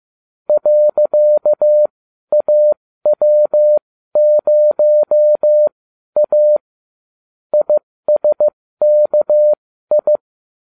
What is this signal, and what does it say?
Morse code, heard as a keyed tone, about 15 wpm, spaced as for 12 wpm, 600 Hz. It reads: .AW0A ISKI